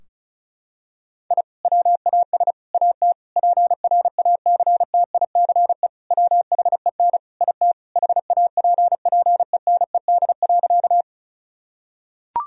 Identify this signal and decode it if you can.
Morse code, heard as a keyed tone, about 36 wpm, 700 Hz. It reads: I WAS AT PRACTICE WHEN IT HAPPENED.
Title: Sentences from Top 1000 Words - Review Part 4 - 35wpm